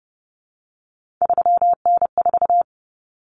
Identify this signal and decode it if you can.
Morse code, heard as a keyed tone, about 30 wpm, 700 Hz. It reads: 3N4
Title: Common Cut Numbers - 3-Digit 30wpm